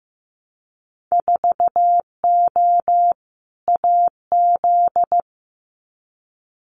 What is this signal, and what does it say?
Morse code, heard as a keyed tone, about 15 wpm, 700 Hz. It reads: 4O AZ
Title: Winter Field Day Exchanges Encore 15wpm